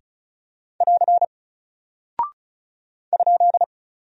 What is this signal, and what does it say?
Morse code, heard as a keyed tone, about 36 wpm, 700 Hz. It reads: <AR> ?
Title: Single Character (Letter:Number:Punctuation:Prosign) Alt 35wpm